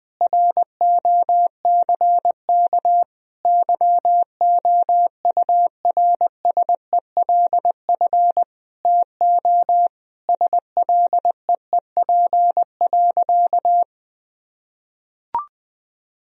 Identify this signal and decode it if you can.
Morse code, heard as a keyed tone, about 20 wpm, 700 Hz. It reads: ROCK YOURSELF TO SLEEP.
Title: Sentences from Top 1000 Words - Review Part 2 - 20wpm